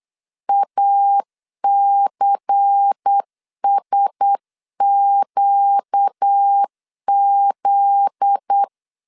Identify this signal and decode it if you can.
Morse code, heard as a keyed tone, about 8 wpm, 800 Hz. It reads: ACSQZ
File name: LEZIONI / 7 GF9Z / 7 GF9Z 8wpm 3213